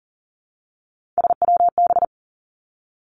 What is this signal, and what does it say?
Morse code, heard as a keyed tone, about 40 wpm, 700 Hz. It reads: SWB
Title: Three Challenging Letter-Number Combinations 40wpm